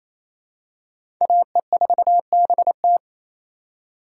Morse code, heard as a keyed tone, about 28 wpm, 700 Hz. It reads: AE4BT
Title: USA Call Signs - 5 Characters - 4x Character Spacing - 28wpm